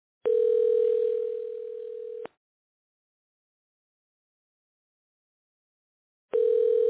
Ringback tone (North America), 440 + 480 Hz.